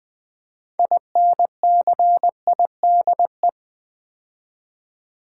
Morse code, heard as a keyed tone, about 20 wpm, 700 Hz. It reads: INCIDE
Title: Build A Word - 3 Letters At A Time - 8-Letter Words - 20wpm